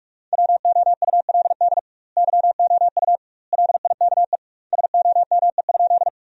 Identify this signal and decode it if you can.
Morse code, heard as a keyed tone, about 47 wpm, 700 Hz. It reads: WOULD YOU LIKE SOME?
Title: 2x Word-Spacing - Sentences from Top 100 Words - 45wpm